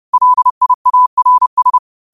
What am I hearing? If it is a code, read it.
Morse code, heard as a keyed tone, about 30 wpm, 1000 Hz. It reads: LIARS